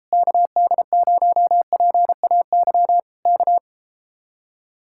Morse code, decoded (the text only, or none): KD0PAY K